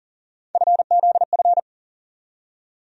Morse code, heard as a keyed tone, about 40 wpm, 700 Hz. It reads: FZF
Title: USA Call Sign Postfixes - 3 Letters 40wpm